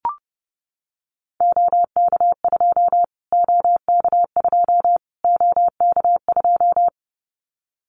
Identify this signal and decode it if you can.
Morse code, heard as a keyed tone, about 30 wpm, 700 Hz. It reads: OK2 OK2 OK2
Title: Callsign Prefixes for Top 25 DXCC - Repeated 3x - 30wpm